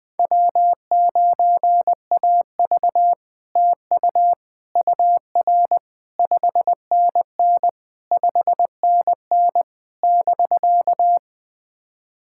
Morse code, heard as a keyed tone, about 20 wpm, 700 Hz. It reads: W9AV TU UR 5NN 5NN <BK>
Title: POTA QSO Practice - 20wpm